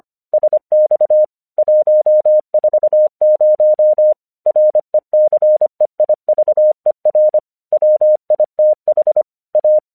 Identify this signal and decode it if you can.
Morse code, heard as a keyed tone, about 26 wpm, 610 Hz. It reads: SX 140 RECEIVER WITH A
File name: 1 English 25wpm / AQC12